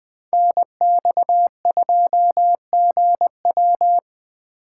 Morse code, heard as a keyed tone, about 20 wpm, 700 Hz. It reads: NX2GW
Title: Speed Racing - USA Call Signs 17wpm